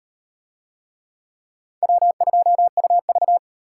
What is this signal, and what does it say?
Morse code, heard as a keyed tone, about 38 wpm, 700 Hz. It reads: W2UV